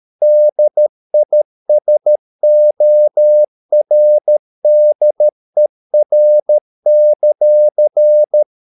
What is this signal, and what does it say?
Morse code, heard as a keyed tone, about 13 wpm, 600 Hz. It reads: DISORDER;